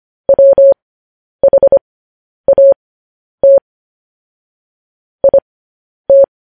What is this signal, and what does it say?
Morse code, heard as a keyed tone, about 25 wpm, spaced as for 10 wpm, 550 Hz. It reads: WHAT IT
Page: News Headlines in Morse Code at 10 WPM